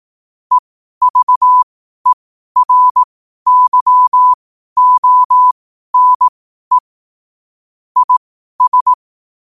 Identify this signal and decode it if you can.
Morse code, heard as a keyed tone, about 18 wpm, spaced as for 12 wpm, 1000 Hz. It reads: EVERYONE IS
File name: qotdfiles / 12wpm